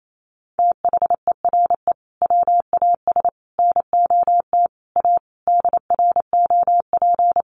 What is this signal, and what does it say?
Morse code, heard as a keyed tone, about 28 wpm, 700 Hz. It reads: THERE WAS NOT A DROP
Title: Sentences from Top 1400 Words 28wpm